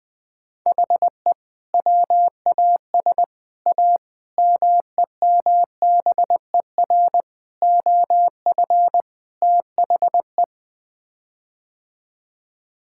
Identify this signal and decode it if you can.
Morse code, heard as a keyed tone, about 20 wpm, 700 Hz. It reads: HE WAS A MEMBER OF THE
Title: Sets of 6 Words - 20wpm